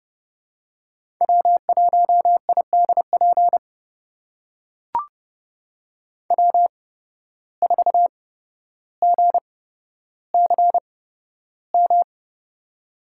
Morse code, heard as a keyed tone, about 30 wpm, spaced as for 13 wpm, 700 Hz. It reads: W1IDP W4GCM
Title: USA Call Signs - 5 Characters - 8x Character Spacing - 30wpm